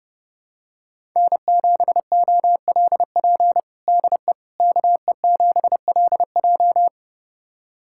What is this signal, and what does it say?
Morse code, heard as a keyed tone, about 30 wpm, 700 Hz. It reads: N7OLP DE KE7LJ
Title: USA Callsign DE Callsign - 5 Characters - 30wpm